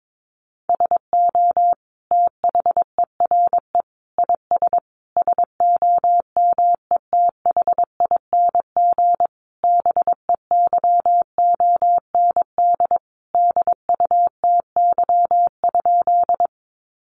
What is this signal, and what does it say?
Morse code, heard as a keyed tone, about 22 wpm, 700 Hz. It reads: SO THERE IS SOMETHING BEYOND DUTY?
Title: Sentences from Top 1500 Words 22wpm